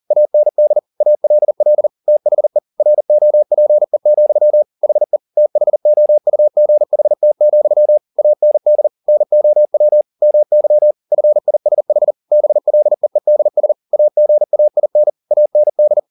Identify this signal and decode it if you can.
Morse code, heard as a keyed tone, about 40 wpm, 600 Hz. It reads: AND ALL THE ROPE, HE THOUGHT, AND NOW MY FISH BLEEDS AGAIN AND
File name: the old man and the sea 2 40wpm